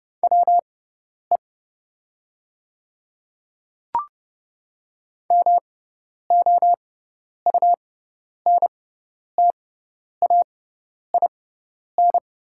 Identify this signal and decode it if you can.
Morse code, heard as a keyed tone, about 30 wpm, spaced as for 10 wpm, 700 Hz. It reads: WE MOUNTAIN
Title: 6x Character Spacing - Top 300 Words - 30wpm